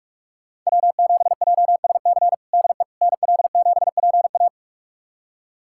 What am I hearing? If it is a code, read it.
Morse code, heard as a keyed tone, about 46 wpm, 700 Hz. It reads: W7JSC DE NL7PA
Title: USA Callsign DE Callsign - 5 Characters - 45wpm